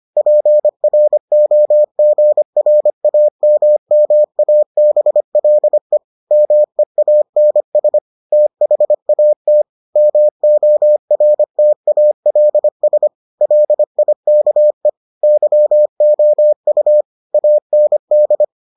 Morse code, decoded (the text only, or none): PROGRAMMABLE MEANS THAT MORTALS LIKE YOU AND